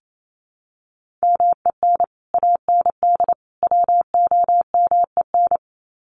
Morse code, heard as a keyed tone, about 28 wpm, 700 Hz. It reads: MEN AND WOMEN
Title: Sets of 3 Words 28wpm